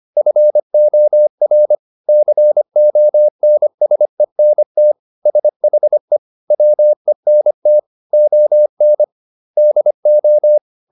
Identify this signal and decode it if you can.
Morse code, heard as a keyed tone, about 25 wpm, 600 Hz. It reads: FOR CONSENT SHE WENT ON DO